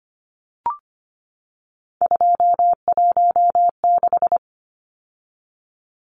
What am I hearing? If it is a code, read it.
Morse code, heard as a keyed tone, about 25 wpm, 700 Hz. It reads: 216
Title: Numbers - 3-Digit 25wpm